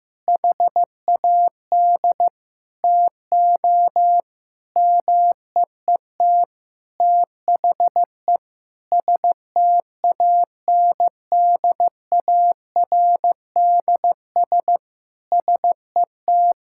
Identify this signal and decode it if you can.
Morse code, heard as a keyed tone, about 15 wpm, 700 Hz. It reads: HAD TO MEET THE STANDARDS SET
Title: Sentences from Top 1000 Words - Review Part 6 - 15wpm